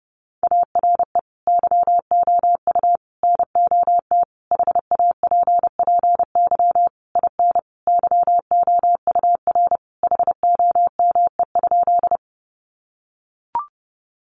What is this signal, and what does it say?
Morse code, heard as a keyed tone, about 28 wpm, 700 Hz. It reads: ARE YOU NOT HAPPY IN YOUR HOME?
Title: Sentences from Top 1000 Words 30wpm